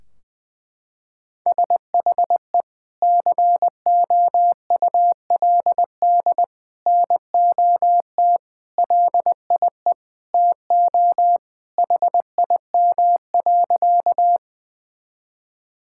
Morse code, heard as a keyed tone, about 21 wpm, 700 Hz. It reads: SHE COULD NOT LIE TO HIM.
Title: Sentences from Top 1600 Words 20wpm